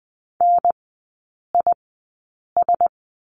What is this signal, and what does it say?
Morse code, heard as a keyed tone, about 20 wpm, spaced as for 8 wpm, 700 Hz. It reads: NIS
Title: 035 Lesson 7e - Letter s - Three Characters 20wpm